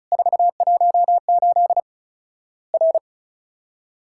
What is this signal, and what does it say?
Morse code, heard as a keyed tone, about 36 wpm, 700 Hz. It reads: 418 R
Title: Worked All Europe DX Contest - QTC Exchange - 35wpm